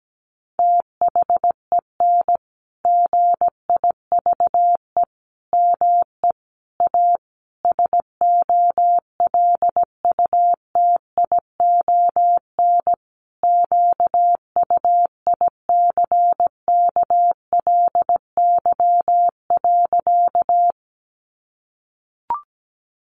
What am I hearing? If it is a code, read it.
Morse code, heard as a keyed tone, about 17 wpm, 700 Hz. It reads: THEN GIVE ME A SOLUTION QUICKLY.
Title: Sentences from Top 1300 Words 17wpm